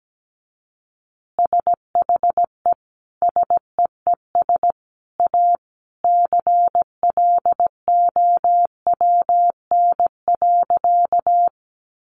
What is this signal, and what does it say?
Morse code, heard as a keyed tone, about 17 wpm, 700 Hz. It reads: SHE SEES A CLOWN.
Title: Easy Phrases 17wpm